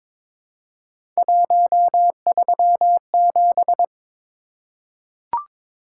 Morse code, heard as a keyed tone, about 22 wpm, 700 Hz. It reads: 137
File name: practice / 22wpm / Three Letter-Number Combinations - 22wpm